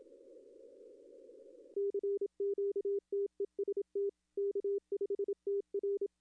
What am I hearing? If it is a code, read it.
Morse code, heard as a keyed tone, about 27 wpm, 400 Hz. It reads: CQTEST K5TR
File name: contests / arrl-160-k5tr-op-w5tn -1300